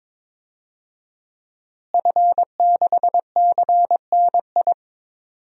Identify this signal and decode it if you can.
Morse code, heard as a keyed tone, about 22 wpm, 700 Hz. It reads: F6CNI
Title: DXCC-100 Call Signs - 5 Characters - Repeated 2x - 22wpm